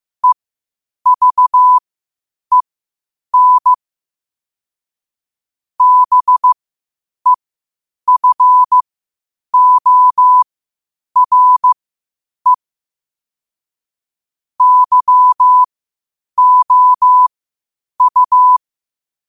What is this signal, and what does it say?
Morse code, heard as a keyed tone, about 15 wpm, spaced as for 8 wpm, 1000 Hz. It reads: EVEN BEFORE YOU
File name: qotdfiles / 8wpm